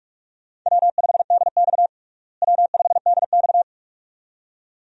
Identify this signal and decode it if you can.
Morse code, heard as a keyed tone, about 47 wpm, 700 Hz. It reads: W5DX W5DX
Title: USA Call Signs - 4 Characters - Repeated 2x - 45wpm